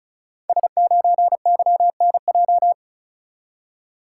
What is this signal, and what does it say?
Morse code, heard as a keyed tone, about 36 wpm, 690 Hz. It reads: S9YNJ
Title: Code Groups - 5 Letter:Numbers - 35wpm